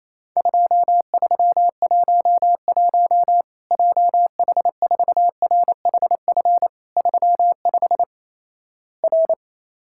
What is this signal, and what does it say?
Morse code, heard as a keyed tone, about 28 wpm, 700 Hz. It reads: 2311 JH4RHF 35 R
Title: Worked All Europe DX Contest - QTC Exchange - 28wpm